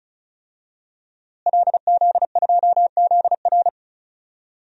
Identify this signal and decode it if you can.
Morse code, heard as a keyed tone, about 34 wpm, 700 Hz. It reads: LZ2ZR
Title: DXCC-100 Call Signs - 5 Characters - Repeated 3x - 35wpm